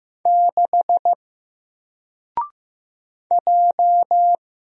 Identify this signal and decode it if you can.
Morse code, heard as a keyed tone, about 15 wpm, 700 Hz. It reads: 6 J
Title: Single Character (Letter:Number:Punctuation:Prosign) 15wpm